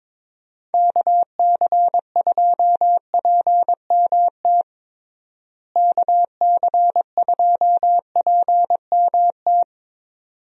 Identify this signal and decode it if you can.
Morse code, heard as a keyed tone, about 21 wpm, 700 Hz. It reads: KC2PMT KC2PMT
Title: USA Call Signs - 6 Characters - Repeated 2x - 22wpm